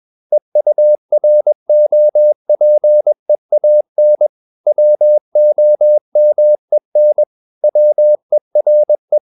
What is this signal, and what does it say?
Morse code, heard as a keyed tone, about 21 wpm, 600 Hz. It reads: EUROPEAN WOMEN WERE
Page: CW med Gnister